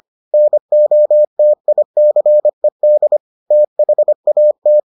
Morse code, heard as a keyed tone, about 25 wpm, 600 Hz. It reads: NOTICED THAT